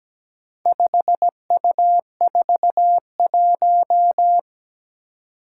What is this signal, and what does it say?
Morse code, heard as a keyed tone, about 17 wpm, 700 Hz. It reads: 5U41
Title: Four Challenging Letter-Number Combinations 17wpm